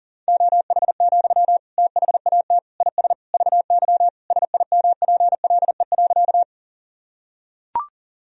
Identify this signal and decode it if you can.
Morse code, heard as a keyed tone, about 40 wpm, 700 Hz. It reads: OH, THAT IS VY SIMPLE.
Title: Sentences from Top 500 Words and Abbreviations 40wpm